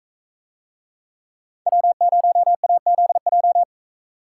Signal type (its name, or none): Morse code